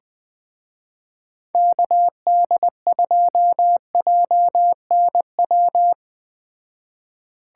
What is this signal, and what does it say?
Morse code, heard as a keyed tone, about 20 wpm, 700 Hz. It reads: KD2JNW
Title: Speed Racing - USA Call Signs 17wpm